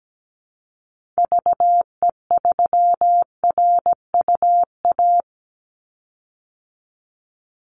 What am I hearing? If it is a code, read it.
Morse code, heard as a keyed tone, about 17 wpm, 700 Hz. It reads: VE3RUA
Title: Call Signs - 17wpm